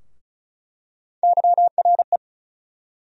Morse code, heard as a keyed tone, about 36 wpm, 700 Hz. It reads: YRE
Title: USA Call Sign Postfixes - 3 Letters 35wpm